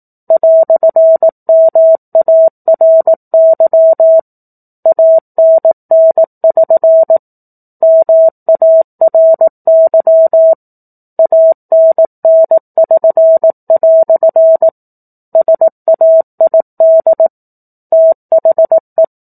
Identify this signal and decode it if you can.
Morse code, heard as a keyed tone, about 18 wpm, 650 Hz. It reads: "MARY ANN<SN> MARY ANN<SN>" SAID THE
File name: 04-Chapter-04 18 WPM